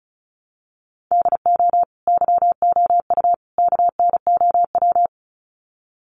Morse code, heard as a keyed tone, about 34 wpm, 700 Hz. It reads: DO YOU KNOW
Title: Sets of 3 Words 35wpm